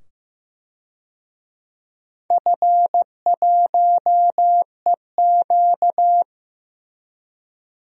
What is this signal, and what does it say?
Morse code, heard as a keyed tone, about 15 wpm, 700 Hz. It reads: F1EQ
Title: DXCC-100 Call Signs - 4 Characters - 15wpm